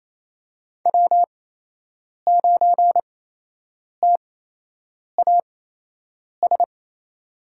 Morse code, heard as a keyed tone, about 28 wpm, spaced as for 8 wpm, 700 Hz. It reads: W9TAS
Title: USA Call Signs - 5 Characters - 8x Character Spacing - 28wpm